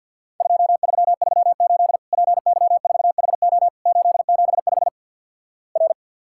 Morse code, heard as a keyed tone, about 51 wpm, 700 Hz. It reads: 2338 PY4HO 875 R
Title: Worked All Europe DX Contest - QTC Exchange - 50wpm